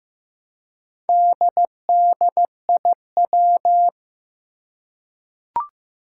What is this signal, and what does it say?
Morse code, heard as a keyed tone, about 15 wpm, 700 Hz. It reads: DDIW
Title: Four Challenging Letter-Number Combinations 15wpm